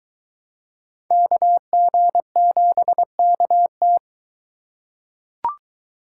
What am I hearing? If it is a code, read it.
Morse code, heard as a keyed tone, about 22 wpm, 700 Hz. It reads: KG7KT